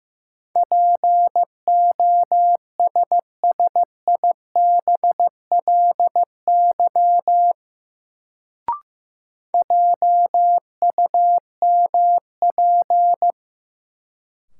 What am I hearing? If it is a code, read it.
Morse code, heard as a keyed tone, about 15 wpm, 700 Hz. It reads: POSSIBLY JUMP
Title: Sentences from Top 1300 Words - New Word Drill Alt - 15wpm